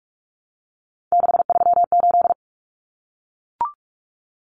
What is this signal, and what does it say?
Morse code, heard as a keyed tone, about 44 wpm, 700 Hz. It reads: 638
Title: Numbers - 3-Digit 45wpm